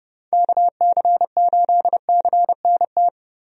Morse code, heard as a keyed tone, about 30 wpm, 700 Hz. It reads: KC8CNT
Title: USA Call Signs - 6 Characters - Repeated 2x - 30wpm